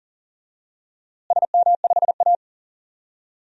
Morse code, heard as a keyed tone, about 40 wpm, 700 Hz. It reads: SM5A
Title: DXCC-100 Call Signs - 4 Characters - Repeated 3x - 40wpm